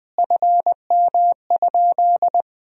Morse code, heard as a keyed tone, about 21 wpm, 700 Hz. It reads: FM?
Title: Sentences from Top 200 Words and Abbreviations 20wpm